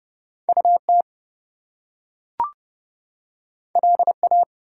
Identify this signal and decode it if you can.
Morse code, heard as a keyed tone, about 31 wpm, 700 Hz. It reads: UT LA
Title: US States and Canadian Provinces 30wpm